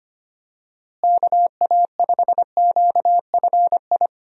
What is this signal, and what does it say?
Morse code, heard as a keyed tone, about 25 wpm, 700 Hz. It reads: KA5QFI
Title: USA Call Signs - 6 Characters - Repeated 3x - 25wpm